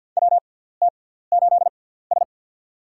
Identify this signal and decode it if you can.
Morse code, heard as a keyed tone, about 48 wpm, spaced as for 17 wpm, 700 Hz. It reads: WT8S
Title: USA Call Signs - 4 Characters - 6x Character Spacing - 50wpm